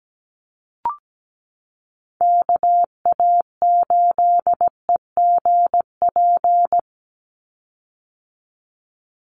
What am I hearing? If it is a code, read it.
Morse code, heard as a keyed tone, about 17 wpm, 700 Hz. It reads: KA8EGP
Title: USA Call Signs 17wpm